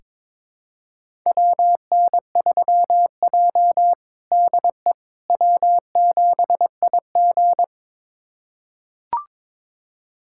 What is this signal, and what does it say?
Morse code, heard as a keyed tone, about 21 wpm, 700 Hz. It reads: WN3J DE W7IG
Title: USA Callsign DE Callsign - 4 Characters - 22wpm